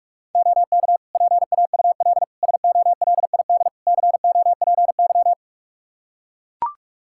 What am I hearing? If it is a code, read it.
Morse code, heard as a keyed tone, about 41 wpm, 700 Hz. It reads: OK PAUL SOLID COPY
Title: K1USN SST QSO Practice 45wpm